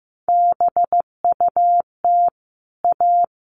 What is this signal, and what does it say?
Morse code, heard as a keyed tone, about 15 wpm, 700 Hz. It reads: BUT A
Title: Sets of 5 Words Encore2 15wpm